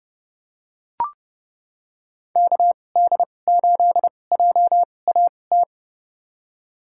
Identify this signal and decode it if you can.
Morse code, heard as a keyed tone, about 30 wpm, spaced as for 22 wpm, 700 Hz. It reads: KD8JAT
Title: USA Call Signs - 2x Character Spacing - 30wpm